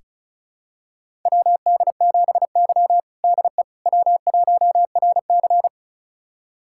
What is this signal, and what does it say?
Morse code, heard as a keyed tone, about 34 wpm, 700 Hz. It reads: WD7Y DE W1RC
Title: USA Callsign DE Callsign - 4 Characters - 35wpm